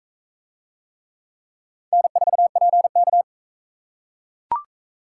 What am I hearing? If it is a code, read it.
Morse code, heard as a keyed tone, about 40 wpm, 700 Hz. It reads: N4PK